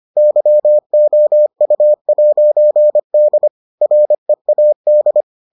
Morse code, heard as a keyed tone, about 25 wpm, 600 Hz. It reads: YOU'D READ